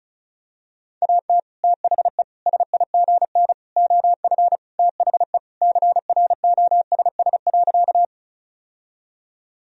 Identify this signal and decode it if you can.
Morse code, heard as a keyed tone, about 36 wpm, 700 Hz. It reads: AT THE SIGN OF THE CROSS.
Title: Sentences from Top 1000 Words - Review Part 1 - 35wpm